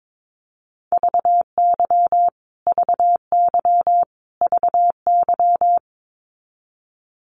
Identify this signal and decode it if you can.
Morse code, heard as a keyed tone, about 22 wpm, 700 Hz. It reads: VY VY VY
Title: Abbreviations Repeated 3x - 22wpm